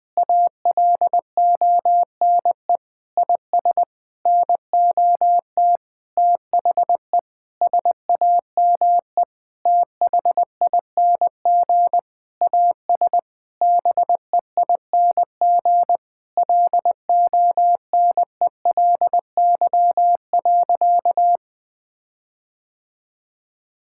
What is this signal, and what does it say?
Morse code, heard as a keyed tone, about 21 wpm, 700 Hz. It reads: ALONE IS NOT THE SAME THING AS BEING LONELY.
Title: Sentences with 6-Word Phrases - 20wpm